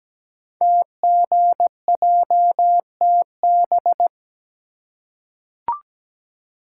Morse code, heard as a keyed tone, about 17 wpm, 700 Hz. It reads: TGJTB
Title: Code Groups - 5 Letters - 17wpm